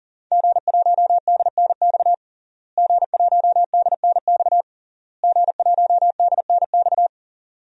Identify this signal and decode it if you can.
Morse code, heard as a keyed tone, about 40 wpm, 700 Hz. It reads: G1DNX G1DNX G1DNX
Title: DXCC-100 Call Signs - 5 Characters - Repeated 3x - 40wpm